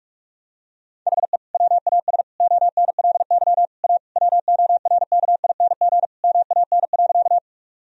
Morse code, heard as a keyed tone, about 41 wpm, 700 Hz. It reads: HE WAS ONLY A WORKING MAN.
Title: Sentences from Top 1000 Words - Review Part 4 - 45wpm